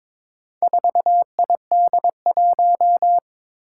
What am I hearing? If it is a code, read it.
Morse code, heard as a keyed tone, about 22 wpm, 700 Hz. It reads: 4ID1
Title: Four Challenging Letter-Number Combinations 22wpm